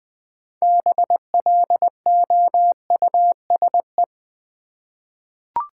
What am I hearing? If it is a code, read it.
Morse code, heard as a keyed tone, about 21 wpm, 700 Hz. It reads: BLOUSE
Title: Difficult Words with Hints - 20wpm